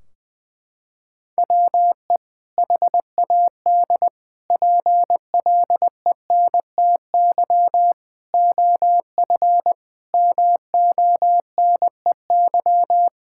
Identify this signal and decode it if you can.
Morse code, heard as a keyed tone, about 21 wpm, 700 Hz. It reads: WE HAD PLENTY OF MONEY
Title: Sentences from Top 1600 Words 20wpm